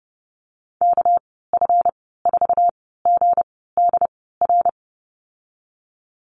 Morse code, heard as a keyed tone, about 30 wpm, spaced as for 17 wpm, 700 Hz. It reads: KF4GDR
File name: Lesson 37g - Letter x - Call Signs 30wpm